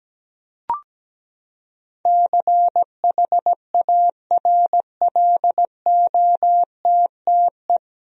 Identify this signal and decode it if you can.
Morse code, heard as a keyed tone, about 17 wpm, 700 Hz. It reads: CHARLOTTE
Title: USA Common Names - 17wpm